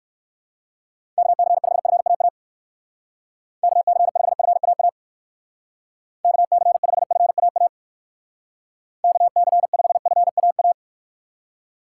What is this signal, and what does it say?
Morse code, heard as a keyed tone, about 64 wpm, 700 Hz. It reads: KC5FAA KC5FAA KC5FAA KC5FAA